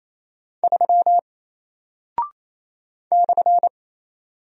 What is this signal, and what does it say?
Morse code, heard as a keyed tone, about 27 wpm, 700 Hz. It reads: 3 /
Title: Single Character (Letter:Number:Punctuation:Prosign) 28wpm